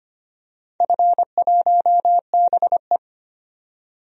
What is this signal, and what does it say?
Morse code, heard as a keyed tone, about 25 wpm, 700 Hz. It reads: F1BE
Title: DXCC-100 Call Signs - 4 Characters - 25wpm